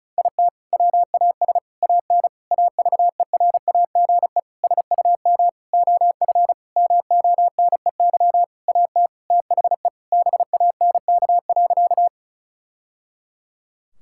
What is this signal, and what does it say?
Morse code, heard as a keyed tone, about 34 wpm, 700 Hz. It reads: IT WAS AN AVERAGE SUM OF MONEY AT THE BANK.
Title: Sentences from Top 1000 Words - Review Part 6 - 35wpm